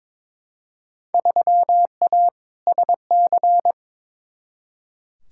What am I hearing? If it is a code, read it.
Morse code, heard as a keyed tone, about 22 wpm, 700 Hz. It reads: 3A SC
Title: Field Day Exchanges - 22wpm